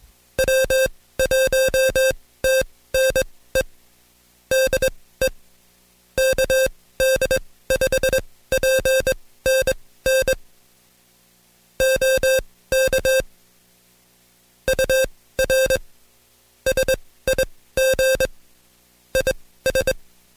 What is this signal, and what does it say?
Morse code, heard as a keyed tone, about 21 wpm, spaced as for 16 wpm, 530 Hz. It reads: W1TNE DE KD5PNN OK UR SIG IS
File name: Code Quick / AudioDownld2 / Seg134-Stepper7 15WPM